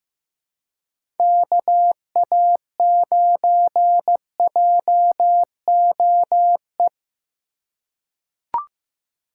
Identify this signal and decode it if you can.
Morse code, heard as a keyed tone, about 15 wpm, 700 Hz. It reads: KA9JOE
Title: USA Call Signs - 6 Characters - 3x Character Spacing - 15wpm